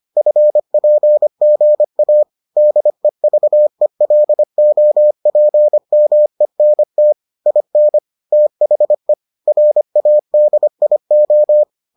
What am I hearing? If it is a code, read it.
Morse code, heard as a keyed tone, about 25 wpm, 600 Hz. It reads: FPGA DEVELOPMENT IN THE RADIO